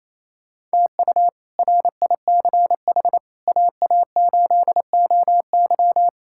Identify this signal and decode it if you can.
Morse code, heard as a keyed tone, about 28 wpm, 700 Hz. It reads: TU RICH AA8OY
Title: K1USN SST QSO Practice - 28wpm